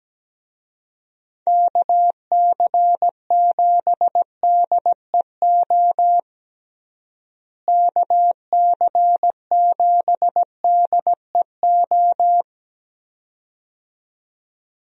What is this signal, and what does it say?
Morse code, heard as a keyed tone, about 17 wpm, 700 Hz. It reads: KC7DEO KC7DEO
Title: USA Call Signs - 6 Characters - Repeated 2x - 17wpm